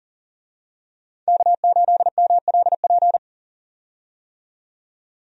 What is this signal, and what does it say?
Morse code, heard as a keyed tone, about 40 wpm, 700 Hz. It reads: K8MLP
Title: USA Call Signs - 5 Characters - 40wpm